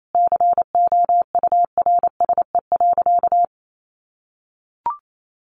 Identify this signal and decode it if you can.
Morse code, heard as a keyed tone, about 28 wpm, 700 Hz. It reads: COURSE.
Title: Sentences from Top 900 Words 28wpm